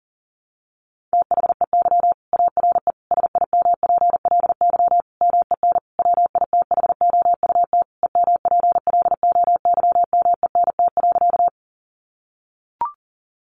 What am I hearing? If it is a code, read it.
Morse code, heard as a keyed tone, about 40 wpm, 700 Hz. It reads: THEY ARE SIMPLY MEN WITHOUT EMPLOYMENT.
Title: Sentences from Top 1500 Words 40wpm